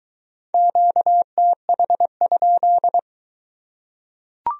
Morse code, heard as a keyed tone, about 23 wpm, 700 Hz. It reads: QTH?